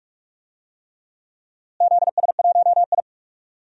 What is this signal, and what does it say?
Morse code, heard as a keyed tone, about 44 wpm, 700 Hz. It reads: ZS1I
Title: DXCC-100 Call Signs - 4 Characters - Repeated 3x - 45wpm